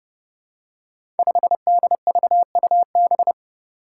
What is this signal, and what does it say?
Morse code, heard as a keyed tone, about 31 wpm, 700 Hz. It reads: 5DVUB
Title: Five Challenging Letter-Number Combinations 30wpm